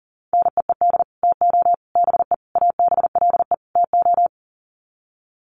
Morse code, heard as a keyed tone, about 40 wpm, 700 Hz. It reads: NEED TO BE ABLE TO
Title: Sets of 5 Words Encore 40wpm